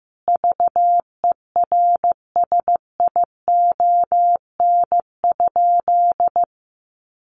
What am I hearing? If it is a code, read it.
Morse code, heard as a keyed tone, about 15 wpm, 700 Hz. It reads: VERSION?
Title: Sentences from Top 1400 Words 15wpm